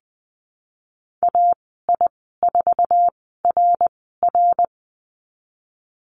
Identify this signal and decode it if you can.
Morse code, heard as a keyed tone, about 20 wpm, 700 Hz. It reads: A I 4 R R
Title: 062 Lesson 11g - Letter h - Call Signs 20wpm